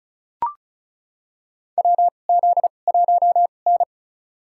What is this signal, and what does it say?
Morse code, heard as a keyed tone, about 36 wpm, spaced as for 25 wpm, 700 Hz. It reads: WZ1N
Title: USA Call Signs - 4 Characters - 2x Character Spacing - 35wpm